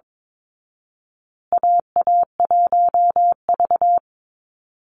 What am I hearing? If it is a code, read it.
Morse code, heard as a keyed tone, about 22 wpm, 700 Hz. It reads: AA1V
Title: Call Signs Repeated 2x - 22wpm